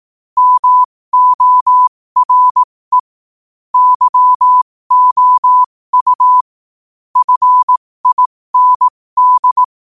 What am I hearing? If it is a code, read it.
Morse code, heard as a keyed tone, about 18 wpm, spaced as for 15 wpm, 1000 Hz. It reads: MORE YOU FIND
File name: qotdfiles / 15wpm